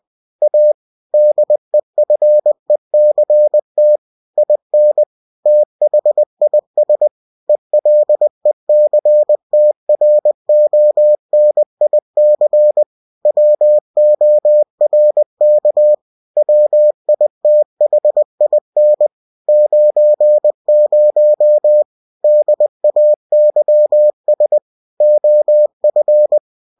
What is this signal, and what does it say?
Morse code, heard as a keyed tone, about 20 wpm, 600 Hz. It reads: A DEFECT IN THIS ELECTRONIC WORK WITHIN 90 DAYS OF
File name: cw-books / winnie-the-pooh / winnie-20wpm-page-0119